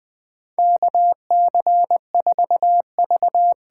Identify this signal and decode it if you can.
Morse code, heard as a keyed tone, about 20 wpm, 700 Hz. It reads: KC4V
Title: USA Call Signs - 6 Characters - 8x Character Spacing - 20wpm